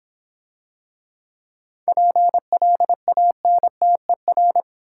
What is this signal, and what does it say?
Morse code, heard as a keyed tone, about 26 wpm, 700 Hz. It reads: PLANTER